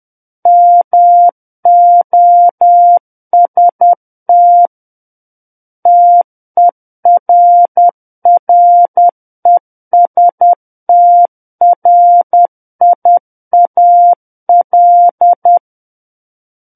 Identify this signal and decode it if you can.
Morse code, heard as a keyed tone, about 10 wpm, 700 Hz. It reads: MOST TERRESTRIAL
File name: Book One / 01-Chapter 1 - 10 WPM